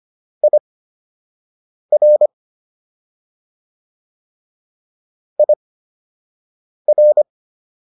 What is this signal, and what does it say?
Morse code, heard as a keyed tone, about 25 wpm, spaced as for 6 wpm, 600 Hz. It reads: IR IR